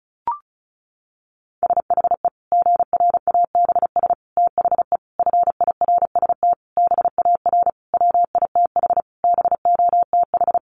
Morse code, heard as a keyed tone, about 34 wpm, 700 Hz. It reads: SHE GRABS THE FIRST BAR WITH BOTH
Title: Easy Phrases 35wpm